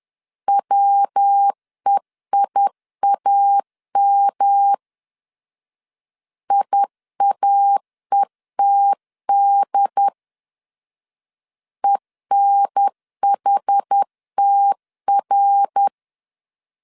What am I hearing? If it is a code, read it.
Morse code, heard as a keyed tone, about 10 wpm, 800 Hz. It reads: WEIAM IAETD ENHTR
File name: CW / DET / 1 3 1500C S B 10wpm 3335